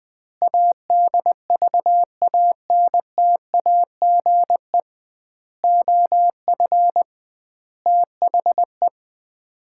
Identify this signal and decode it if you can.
Morse code, heard as a keyed tone, about 20 wpm, 700 Hz. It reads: ADVANTAGE OF THE
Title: 2x Word-Spacing - Sets of 5 Words - 20wpm